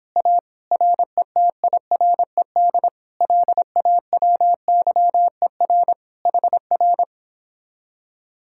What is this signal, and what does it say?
Morse code, heard as a keyed tone, about 31 wpm, 700 Hz. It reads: A RETIRED LAWYER HR